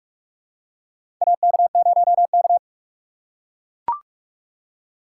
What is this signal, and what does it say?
Morse code, heard as a keyed tone, about 42 wpm, 700 Hz. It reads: AK0K
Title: USA Call Signs - 4 Characters - 6x Character Spacing - 45wpm